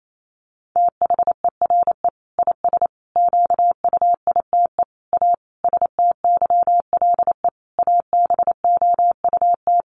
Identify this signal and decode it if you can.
Morse code, heard as a keyed tone, about 28 wpm, 700 Hz. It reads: THERE IS QUITE A STYLE ABOUT
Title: Sentences from Top 800 Words 28wpm